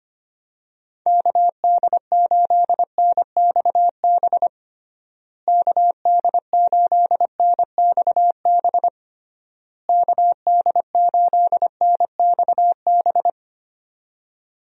Morse code, heard as a keyed tone, about 25 wpm, 700 Hz. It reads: KD8NXB KD8NXB KD8NXB
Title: USA Call Signs - 6 Characters - Repeated 3x - 25wpm